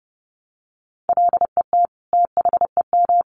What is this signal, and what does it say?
Morse code, heard as a keyed tone, about 30 wpm, 700 Hz. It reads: LET THEM
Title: Sentences from Top 300 Words 30wpm